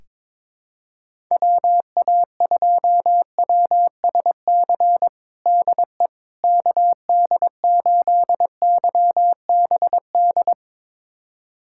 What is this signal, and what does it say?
Morse code, heard as a keyed tone, about 22 wpm, 700 Hz. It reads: WA2WSC DE KD8YBD
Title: USA Callsign DE Callsign - 6 Characters - 22wpm